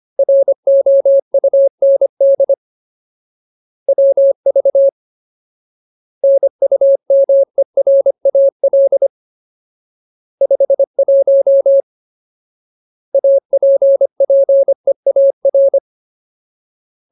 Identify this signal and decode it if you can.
Morse code, heard as a keyed tone, about 25 wpm, 550 Hz. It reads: ROUND WV NUMERAL 51 APPEAR
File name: Ultimate Word List / 25WPM Ultimate